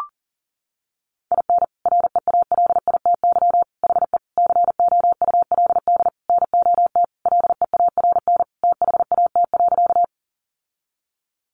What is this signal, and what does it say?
Morse code, heard as a keyed tone, about 40 wpm, 700 Hz. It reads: IN REALITY HE COULD NOT LEARN THAT.
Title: Sentences from Top 1200 Words 40wpm